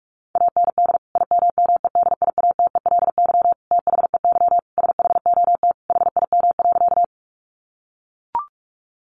Morse code, heard as a keyed tone, about 44 wpm, 700 Hz. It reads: AND IMMEDIATELY THEY SHOT HIM.
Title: Sentences from Top 1100 Words 45wpm